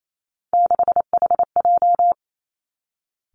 Morse code, heard as a keyed tone, about 28 wpm, 700 Hz. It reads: 6HJ
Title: Three Challenging Letter-Number Combinations 28wpm